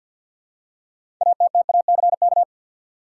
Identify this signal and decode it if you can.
Morse code, heard as a keyed tone, about 52 wpm, 700 Hz. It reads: ATTACK
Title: Six Letter Words - 50wpm